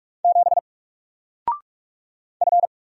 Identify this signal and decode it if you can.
Morse code, heard as a keyed tone, about 46 wpm, 700 Hz. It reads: 7 F
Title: Single Letter-Number 45wpm